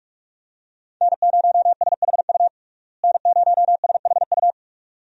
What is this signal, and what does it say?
Morse code, heard as a keyed tone, about 46 wpm, 700 Hz. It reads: N0SHU N0SHU
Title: USA Call Signs - 5 Characters - Repeated 2x - 45wpm